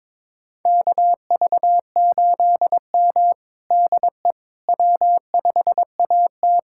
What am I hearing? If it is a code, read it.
Morse code, heard as a keyed tone, about 23 wpm, 700 Hz. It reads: KV8M DE W5AT
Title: USA Callsign DE Callsign - 4 Characters - 22wpm